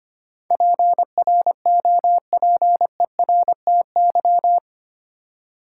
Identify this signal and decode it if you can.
Morse code, heard as a keyed tone, about 25 wpm, 700 Hz. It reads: PROPERTY
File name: 8 Letter Words - First 3 Letters Given - 25wpm